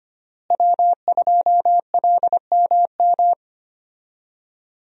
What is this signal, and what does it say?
Morse code, heard as a keyed tone, about 25 wpm, 700 Hz. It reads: W2LMM
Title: USA Call Signs - 5 Characters - Repeated 2x - 25wpm